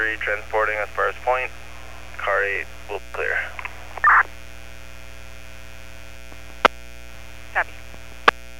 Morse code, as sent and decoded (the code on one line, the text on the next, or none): none